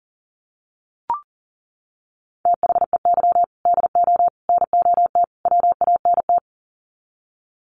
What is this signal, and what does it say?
Morse code, heard as a keyed tone, about 40 wpm, 700 Hz. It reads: THEY DO NOT WANT
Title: Sets of 4 Words 40wpm